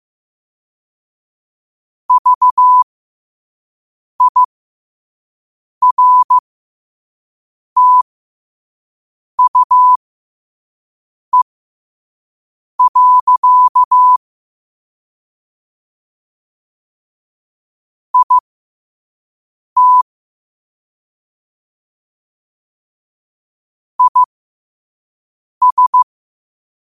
Morse code, heard as a keyed tone, about 15 wpm, spaced as for 5 wpm, 1000 Hz. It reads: VIRTUE. IT IS
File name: qotdfiles / 5wpm